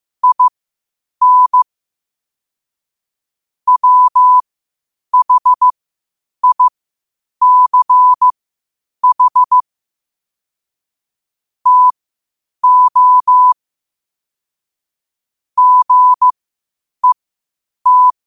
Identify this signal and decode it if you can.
Morse code, heard as a keyed tone, about 15 wpm, spaced as for 8 wpm, 1000 Hz. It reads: IN WHICH TO GET
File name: qotdfiles / 8wpm